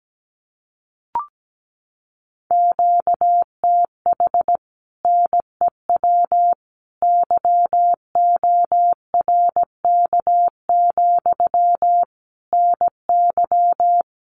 Morse code, heard as a keyed tone, about 17 wpm, 700 Hz. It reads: QTH NEW YORK, NY
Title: Common QSO Phrases Encore 17wpm